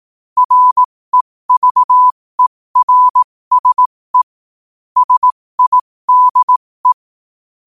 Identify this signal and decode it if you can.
Morse code, heard as a keyed tone, about 18 wpm, spaced as for 15 wpm, 1000 Hz. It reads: REVERSE SIDE
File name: qotdfiles / 15wpm